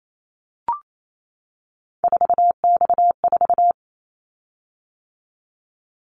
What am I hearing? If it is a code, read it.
Morse code, heard as a keyed tone, about 28 wpm, 700 Hz. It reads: 4X4
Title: Callsign Prefixes for Top 50 DXCC - 28wpm